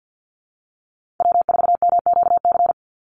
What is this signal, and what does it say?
Morse code, heard as a keyed tone, about 50 wpm, 700 Hz. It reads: W4MQC
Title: Call Signs - 50wpm